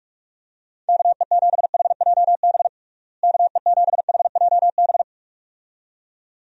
Morse code, heard as a keyed tone, about 46 wpm, 700 Hz. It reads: KE7HJB KE7HJB
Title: USA Call Signs - 6 Characters - Repeated 2x - 45wpm